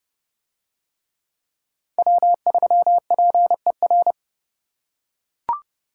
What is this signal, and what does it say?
Morse code, heard as a keyed tone, about 29 wpm, 700 Hz. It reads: W3PER